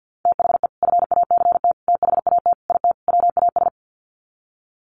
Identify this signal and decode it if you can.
Morse code, heard as a keyed tone, about 50 wpm, 700 Hz. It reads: THE FACT THAT IT WAS
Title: Sets of 5 Words Encore2 50wpm